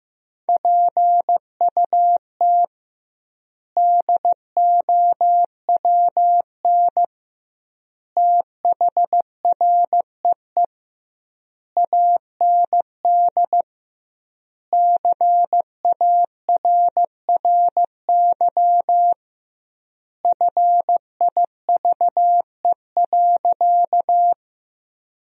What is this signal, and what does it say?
Morse code, heard as a keyed tone, about 15 wpm, 700 Hz. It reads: PUT DOWN THREE AND CARRY FIVE.
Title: 2x Word-Spacing - Sentences from Top 400 Words - 15wpm